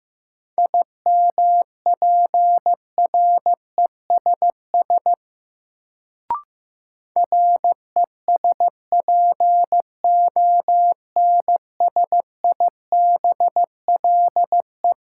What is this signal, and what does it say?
Morse code, heard as a keyed tone, about 15 wpm, 700 Hz. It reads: IMPRESS RESPONSIBLE
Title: Sentences from Top 1600 Words - New Word Drill Alt - 15wpm